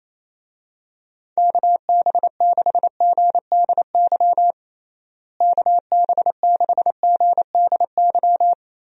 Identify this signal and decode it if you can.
Morse code, heard as a keyed tone, about 28 wpm, 700 Hz. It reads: KB6GDY KB6GDY
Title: USA Call Signs - 6 Characters - Repeated 3x - 28wpm